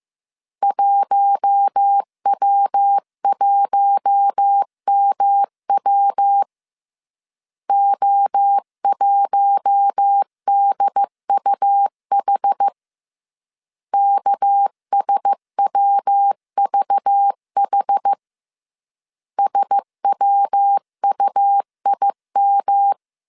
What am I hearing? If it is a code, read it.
Morse code, heard as a keyed tone, about 15 wpm, 800 Hz. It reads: 1W1MW O1DUH KSWVH SWUIM
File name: CW / DET / 1 6 1500C S C 14wpm 2951